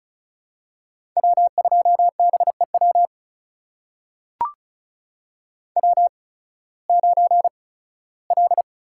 Morse code, heard as a keyed tone, about 34 wpm, spaced as for 18 wpm, 700 Hz. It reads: W2BEW W9L
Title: USA Call Signs - 5 Characters - 8x Character Spacing - 35wpm